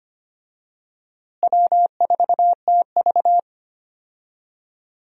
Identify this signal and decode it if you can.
Morse code, heard as a keyed tone, about 25 wpm, 700 Hz. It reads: W4TV
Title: USA Call Signs - 4 Characters - Repeated 2x - 25wpm